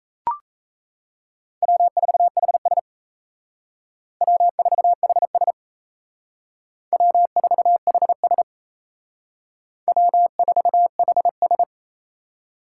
Morse code, heard as a keyed tone, about 38 wpm, 700 Hz. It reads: W4HS W4HS W4HS W4HS